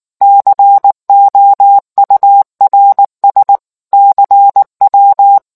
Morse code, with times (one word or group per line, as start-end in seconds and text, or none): 0.2-3.6 COURS
3.9-5.4 CW